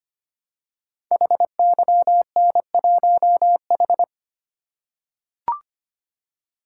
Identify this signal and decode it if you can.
Morse code, heard as a keyed tone, about 25 wpm, 700 Hz. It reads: HYN1H
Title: Code Groups - 5 Letter:Numbers - 25wpm